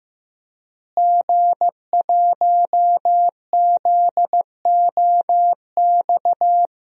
Morse code, heard as a keyed tone, about 15 wpm, 700 Hz. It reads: G1ZOX
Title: DXCC-100 Call Signs - 5 Characters - Repeated 3x - 15wpm